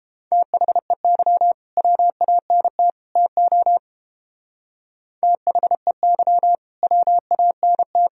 Morse code, heard as a keyed tone, about 29 wpm, 700 Hz. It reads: THEY WANT TO THEY WANT